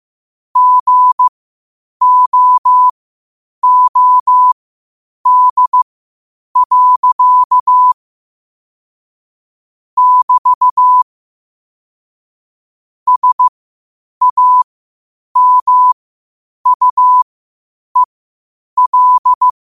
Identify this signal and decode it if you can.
Morse code, heard as a keyed tone, about 15 wpm, spaced as for 8 wpm, 1000 Hz. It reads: GOOD. <BT> SAMUEL
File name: qotdfiles / 8wpm